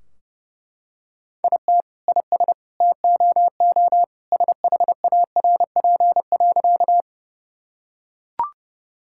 Morse code, heard as a keyed tone, about 30 wpm, 700 Hz. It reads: IT IS TOO SHARP.
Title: Sentences from Top 1600 Words 30wpm